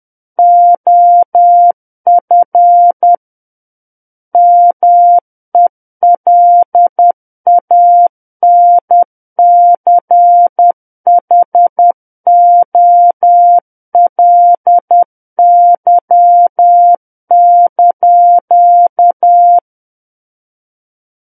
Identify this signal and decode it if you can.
Morse code, heard as a keyed tone, about 10 wpm, 700 Hz. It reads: OF MELANCHOLY)
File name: Book One / 01-Chapter 1 - 10 WPM